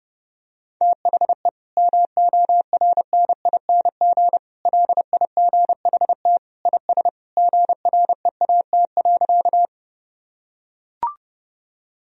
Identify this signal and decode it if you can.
Morse code, heard as a keyed tone, about 30 wpm, 700 Hz. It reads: THE MORNING LIGHT IS GREAT.
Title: Sentences from Top 1000 Words - Review Part 1 - 30wpm